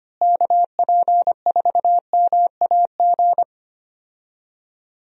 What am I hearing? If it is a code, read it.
Morse code, heard as a keyed tone, about 25 wpm, 700 Hz. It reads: KP4MAG
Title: USA Call Signs - 6 Characters - Repeated 3x - 25wpm